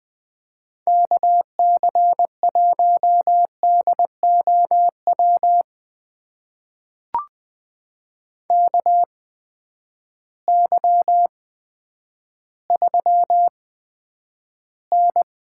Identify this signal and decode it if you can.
Morse code, heard as a keyed tone, about 20 wpm, spaced as for 10 wpm, 700 Hz. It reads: KC1DOW KY3N
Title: USA Call Signs - 8x Character Spacing - 20wpm